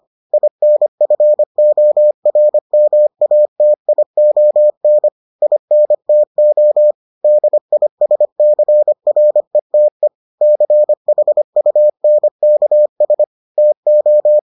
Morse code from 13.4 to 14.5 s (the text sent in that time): TO